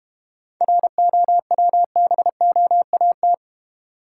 Morse code, heard as a keyed tone, about 32 wpm, 700 Hz. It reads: ROWBOAT